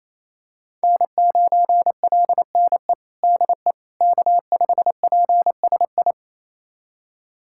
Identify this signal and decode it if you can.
Morse code, heard as a keyed tone, about 28 wpm, 700 Hz. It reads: N9LNE DE K5PSI
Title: USA Callsign DE Callsign - 5 Characters - 28wpm